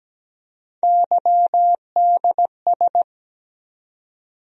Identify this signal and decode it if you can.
Morse code, heard as a keyed tone, about 17 wpm, 700 Hz. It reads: YDS
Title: USA Call Sign Postfixes - 3 Letters 17wpm